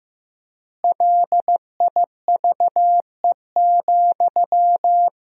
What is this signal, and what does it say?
Morse code, heard as a keyed tone, about 15 wpm, 700 Hz. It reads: LIVE,
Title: 3x Word-Spacing - Sentences from Top 200 Words - 15wpm